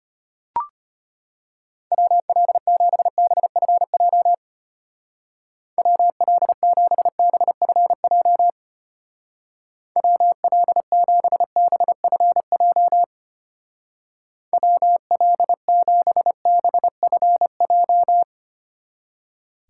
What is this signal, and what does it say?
Morse code, heard as a keyed tone, about 32 wpm, 700 Hz. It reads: WL7BFJ WL7BFJ WL7BFJ WL7BFJ